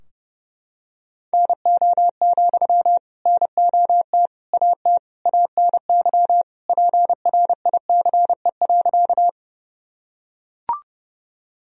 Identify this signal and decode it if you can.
Morse code, heard as a keyed tone, about 32 wpm, 700 Hz. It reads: NO, NOT AT ANY PRICE.
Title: Sentences from Top 1000 Words - Review Part 4 - 30wpm